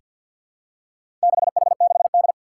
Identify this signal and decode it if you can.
Morse code, heard as a keyed tone, about 47 wpm, 700 Hz. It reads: 6H6D
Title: Four Challenging Letter-Number Combinations 50wpm